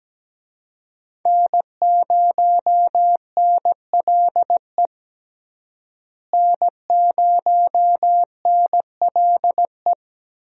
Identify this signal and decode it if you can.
Morse code, heard as a keyed tone, about 17 wpm, 700 Hz. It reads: N0NLE N0NLE
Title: USA Call Signs - 5 Characters - Repeated 3x - 17wpm